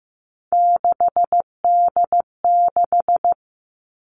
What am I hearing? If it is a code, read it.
Morse code, heard as a keyed tone, about 15 wpm, 700 Hz. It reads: 6D6
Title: Three Challenging Letter-Number Combinations 15wpm